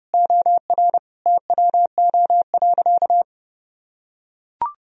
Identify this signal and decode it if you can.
Morse code, heard as a keyed tone, about 30 wpm, 700 Hz. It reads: OR TWO.
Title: Sentences from Top 1000 Words - Review Part 1 - 30wpm